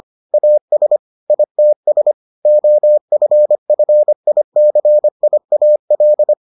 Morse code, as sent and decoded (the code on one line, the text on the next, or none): .- ... / .. - ... / --- ..-. ..-. .. -.-. .. .- .-..
AS ITS OFFICIAL